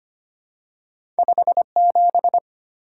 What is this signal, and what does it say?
Morse code, heard as a keyed tone, about 25 wpm, 690 Hz. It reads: 57
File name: Two Letter-Number Combinations - 25wpm